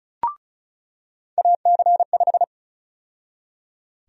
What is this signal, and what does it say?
Morse code, heard as a keyed tone, about 34 wpm, 700 Hz. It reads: AC5
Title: USA Call Sign Prefixes - 35wpm